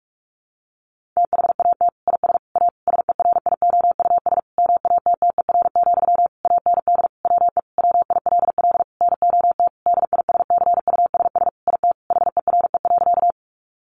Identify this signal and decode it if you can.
Morse code, heard as a keyed tone, about 44 wpm, 700 Hz. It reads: THAT IS A SERIOUS MATTER, AND WE WILL NOT DISCUSS IT HERE.
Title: Sentences from Top 1100 Words 45wpm